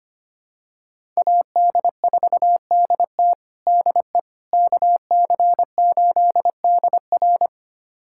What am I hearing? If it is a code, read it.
Morse code, heard as a keyed tone, about 25 wpm, 700 Hz. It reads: AD4DT DE KC8DR
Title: USA Callsign DE Callsign - 5 Characters - 25wpm